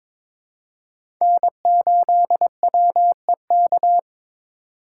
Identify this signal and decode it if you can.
Morse code, heard as a keyed tone, about 22 wpm, 700 Hz. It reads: N8WEK
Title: USA Call Signs - 5 Characters - 4x Character Spacing - 22wpm